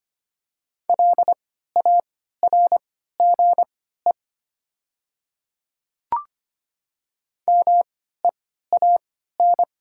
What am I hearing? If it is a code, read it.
Morse code, heard as a keyed tone, about 25 wpm, spaced as for 14 wpm, 700 Hz. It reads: LARGE MEAN
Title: 3x Character Spacing - Top 500 Words - 25wpm